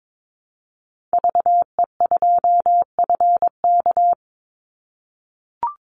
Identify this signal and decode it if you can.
Morse code, heard as a keyed tone, about 22 wpm, 700 Hz. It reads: VE2FK
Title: Call Signs - 22wpm